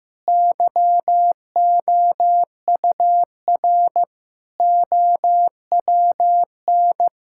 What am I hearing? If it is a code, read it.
Morse code, heard as a keyed tone, about 15 wpm, 700 Hz. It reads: YOUR OWN
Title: Sentences from Top 1000 Words - Review Part 5 - 15wpm